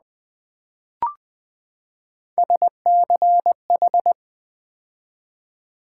Morse code, heard as a keyed tone, about 20 wpm, 700 Hz. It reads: SCH
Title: Build A Word - 3 Letters At A Time - 6-Letter Words - 20wpm